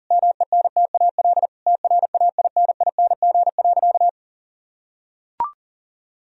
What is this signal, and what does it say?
Morse code, heard as a keyed tone, about 40 wpm, 700 Hz. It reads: MENTAL TRAINING.
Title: Sentences from Top 1000 Words - Review Part 4 - 40wpm